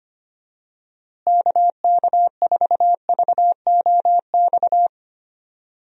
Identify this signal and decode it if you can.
Morse code, heard as a keyed tone, about 25 wpm, 700 Hz. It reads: KK4VOX
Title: USA Call Signs - 6 Characters - Repeated 2x - 25wpm